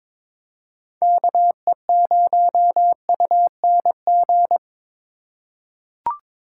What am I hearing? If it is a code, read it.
Morse code, heard as a keyed tone, about 23 wpm, 700 Hz. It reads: KE0UNG
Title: USA Call Signs - 6 Characters - 4x Character Spacing - 22wpm